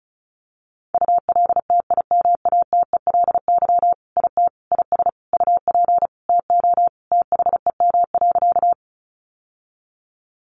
Morse code, heard as a keyed tone, about 34 wpm, 700 Hz. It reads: ULTIMATELY IT IS UP TO THEM.
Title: Sentences from Top 1200 Words 35wpm